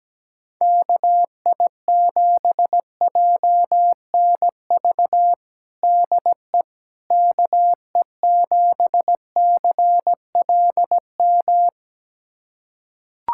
Morse code, heard as a keyed tone, about 17 wpm, 700 Hz. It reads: KI7JNV DE KE7CLM
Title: USA Callsign DE Callsign - 6 Characters - 17wpm